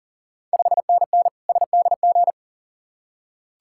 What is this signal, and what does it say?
Morse code, heard as a keyed tone, about 37 wpm, 690 Hz. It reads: 5NN SDG
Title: Ontario QSO Party Exchange 40wpm